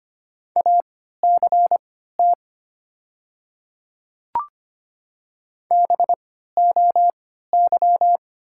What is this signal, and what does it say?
Morse code, heard as a keyed tone, about 25 wpm, spaced as for 14 wpm, 700 Hz. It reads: ACT BOY
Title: 3x Character Spacing - Top 200 Words - 25wpm